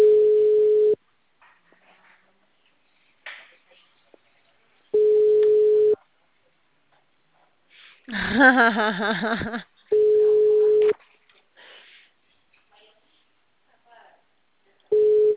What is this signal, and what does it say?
Ringback tone (Europe, 425 Hz), with speech.
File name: out- -2005- - - .17